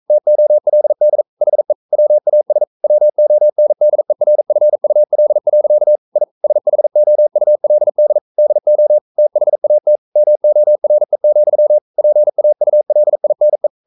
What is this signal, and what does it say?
Morse code, heard as a keyed tone, about 42 wpm, 600 Hz. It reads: TOLD HE WAS WONDERFUL. I SHOULD DO THAT MORE, PAULINE
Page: Romanen Never af Ken Follet på engelsk.